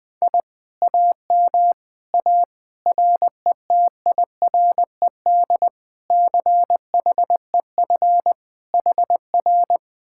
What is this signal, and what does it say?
Morse code, heard as a keyed tone, about 21 wpm, 700 Hz. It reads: I AM A RETIRED CHEF HR
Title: Speed-Racing Common QSO Phrases - 20wpm